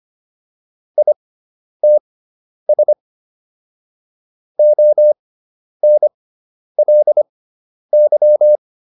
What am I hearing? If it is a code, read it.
Morse code, heard as a keyed tone, about 25 wpm, spaced as for 10 wpm, 600 Hz. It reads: ITS ONLY